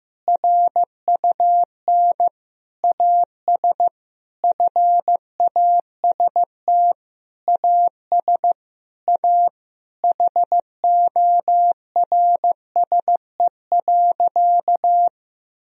Morse code, heard as a keyed tone, about 15 wpm, 700 Hz. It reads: RUN AS FAST AS A HORSE.
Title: Sentences from Top 1000 Words - Review Part 2 - 15wpm